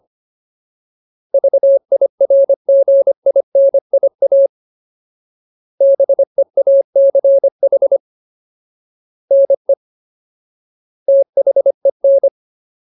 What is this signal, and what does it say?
Morse code, heard as a keyed tone, about 25 wpm, 550 Hz. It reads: VIRGINIA BEACH NE THEN
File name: Ultimate Word List / 25WPM Ultimate